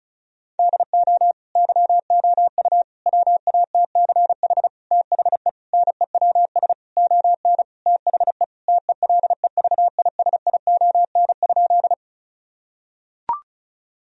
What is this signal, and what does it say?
Morse code, heard as a keyed tone, about 34 wpm, 700 Hz. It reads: DO YOU WATCH THE NEWS ON THE TELEVISION?
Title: Sentences from Top 1000 Words - Review Part 4 - 35wpm